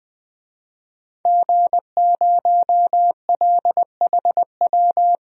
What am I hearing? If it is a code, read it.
Morse code, heard as a keyed tone, about 20 wpm, 700 Hz. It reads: G0LHW
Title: DXCC-100 Call Signs - 5 Characters - 20wpm